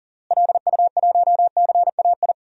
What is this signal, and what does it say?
Morse code, heard as a keyed tone, about 40 wpm, 700 Hz. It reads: LU1CAI
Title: DXCC-100 Call Signs - 6 Characters - Repeated 3x - 40wpm